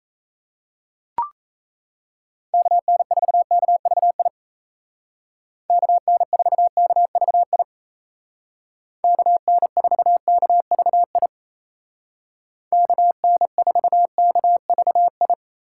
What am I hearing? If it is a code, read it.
Morse code, heard as a keyed tone, about 37 wpm, 700 Hz. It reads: KN4KVI KN4KVI KN4KVI KN4KVI